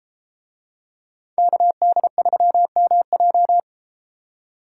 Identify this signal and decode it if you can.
Morse code, heard as a keyed tone, about 33 wpm, 700 Hz. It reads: KD3MJ